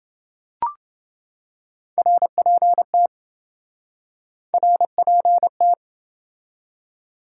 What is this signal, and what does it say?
Morse code, heard as a keyed tone, about 28 wpm, 700 Hz. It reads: RPT RPT